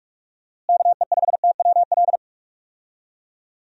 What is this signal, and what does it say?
Morse code, heard as a keyed tone, about 47 wpm, 700 Hz. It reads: KE5TWL
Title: USA Call Signs - 6 Characters - Repeated 3x - 45wpm